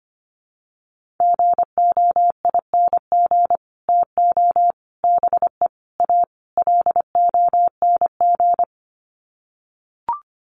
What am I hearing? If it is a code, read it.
Morse code, heard as a keyed tone, about 25 wpm, 700 Hz. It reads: GOING TO BE A LONG
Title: Sets of 5 Words Encore2 25wpm